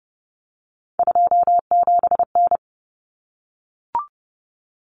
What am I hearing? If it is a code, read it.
Morse code, heard as a keyed tone, about 30 wpm, 700 Hz. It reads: 27N
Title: Common Cut Numbers - 3-Digit 30wpm